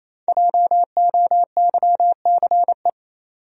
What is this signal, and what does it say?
Morse code, heard as a keyed tone, about 28 wpm, 700 Hz. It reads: JOYCE
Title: USA Common Names - 28wpm